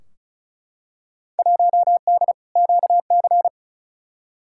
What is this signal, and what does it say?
Morse code, heard as a keyed tone, about 36 wpm, 700 Hz. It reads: 1D QC
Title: Field Day Exchanges - 35wpm